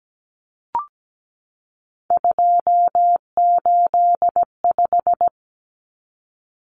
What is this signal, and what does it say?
Morse code, heard as a keyed tone, about 17 wpm, 700 Hz. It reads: 285
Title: Common Cut Numbers - 3-Digit 17wpm